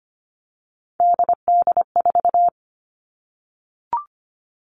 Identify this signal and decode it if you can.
Morse code, heard as a keyed tone, about 25 wpm, 700 Hz. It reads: DD4
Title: Three Challenging Letter-Number Combinations 25wpm